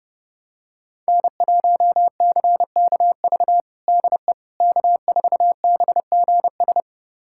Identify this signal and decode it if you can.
Morse code, heard as a keyed tone, about 32 wpm, 700 Hz. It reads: N1CKV DE K4BGS
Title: USA Callsign DE Callsign - 5 Characters - 30wpm